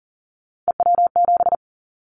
Morse code, heard as a keyed tone, about 40 wpm, 700 Hz. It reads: EW7
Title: Callsign Prefixes for Top 50 DXCC - 40wpm